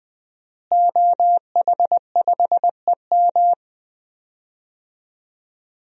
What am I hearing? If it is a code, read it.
Morse code, heard as a keyed tone, about 20 wpm, 700 Hz. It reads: OH5EM
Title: DXCC-100 Call Signs - 5 Characters - Repeated 3x - 20wpm